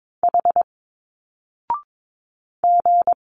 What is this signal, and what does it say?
Morse code, heard as a keyed tone, about 22 wpm, 700 Hz. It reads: H G
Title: Single Letters 22wpm